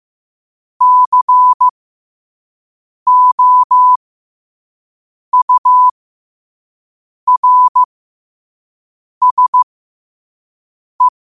Morse code, heard as a keyed tone, about 15 wpm, spaced as for 5 wpm, 1000 Hz. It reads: COURSE
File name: qotdfiles / 5wpm